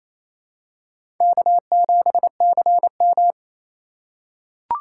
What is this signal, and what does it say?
Morse code, heard as a keyed tone, about 28 wpm, 700 Hz. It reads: K7CM
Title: USA Call Signs - 4 Characters - 2x Character Spacing - 28wpm